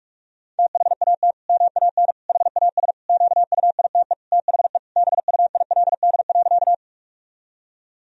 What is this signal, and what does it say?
Morse code, heard as a keyed tone, about 46 wpm, 700 Hz. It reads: THAT MAN HAS QUITE THE BUILD.
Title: Sentences from Top 1000 Words - Review Part 1 - 45wpm